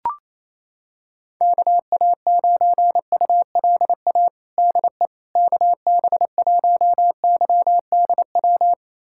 Morse code, heard as a keyed tone, about 27 wpm, 700 Hz. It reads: KA9ULA DE KB1YDW
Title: USA Callsign DE Callsign - 6 Characters - 28wpm